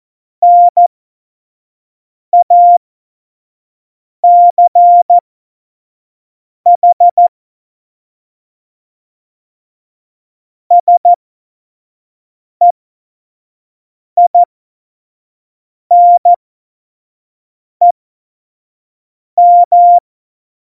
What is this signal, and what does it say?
Morse code, heard as a keyed tone, about 14 wpm, spaced as for 5 wpm, 700 Hz. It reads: NACH SEINEM
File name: Morsekurs / Witze / Witz 29. September 2020, 25 F70